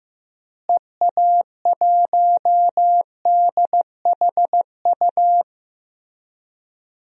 Morse code, heard as a keyed tone, about 16 wpm, 700 Hz. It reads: EA1DHU
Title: DXCC-100 Call Signs - 6 Characters - Repeated 3x - 15wpm